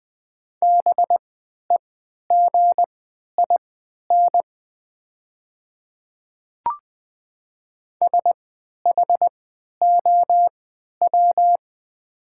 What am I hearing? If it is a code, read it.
Morse code, heard as a keyed tone, about 20 wpm, spaced as for 11 wpm, 700 Hz. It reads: BEGIN SHOW
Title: 3x Character Spacing - Top 300 Words - 20wpm